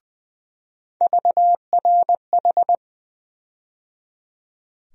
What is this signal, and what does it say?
Morse code, heard as a keyed tone, about 20 wpm, 700 Hz. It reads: VRH
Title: USA Call Sign Postfixes - 3 Letters 20wpm